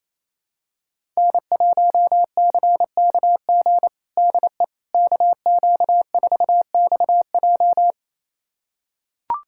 Morse code, heard as a keyed tone, about 27 wpm, 700 Hz. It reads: N1CKG DE KQ4XJ
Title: USA Callsign DE Callsign - 5 Characters - 28wpm